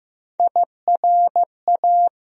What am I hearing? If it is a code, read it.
Morse code, heard as a keyed tone, about 15 wpm, 700 Hz. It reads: IRA
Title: USA Call Sign Postfixes - 3 Letters 15wpm